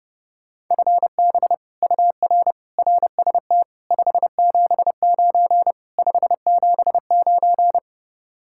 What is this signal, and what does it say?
Morse code, heard as a keyed tone, about 31 wpm, 700 Hz. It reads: FB UR RST 579 579